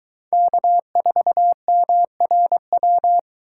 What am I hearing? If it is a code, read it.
Morse code, heard as a keyed tone, about 22 wpm, 700 Hz. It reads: K4MRW